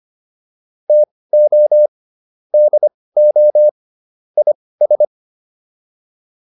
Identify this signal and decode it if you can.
Morse code, heard as a keyed tone, about 25 wpm, spaced as for 18 wpm, 600 Hz. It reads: TO DO IS